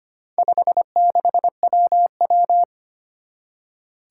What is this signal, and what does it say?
Morse code, heard as a keyed tone, about 25 wpm, 700 Hz. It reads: 56WW
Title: Four Challenging Letter-Number Combinations 25wpm